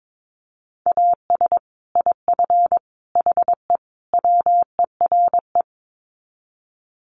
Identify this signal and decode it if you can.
Morse code, heard as a keyed tone, about 22 wpm, 700 Hz. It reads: AS IF HE WERE
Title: Sets of 4 Words 22wpm